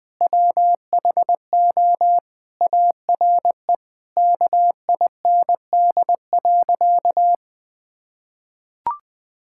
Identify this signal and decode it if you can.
Morse code, heard as a keyed tone, about 21 wpm, 700 Hz. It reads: WHO ARE KIND.
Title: Sentences with 8-Word Phrases - 20wpm